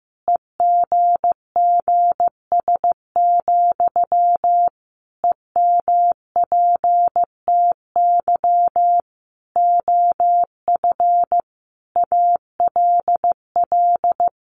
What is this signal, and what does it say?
Morse code, heard as a keyed tone, about 15 wpm, 700 Hz. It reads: EGGS, EMPTY OF ALL
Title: A Sound Of Thunder 15wpm